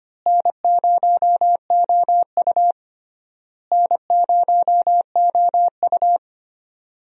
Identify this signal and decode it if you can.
Morse code, heard as a keyed tone, about 25 wpm, 700 Hz. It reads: N0OU N0OU
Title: USA Call Signs - 4 Characters - Repeated 3x - 25wpm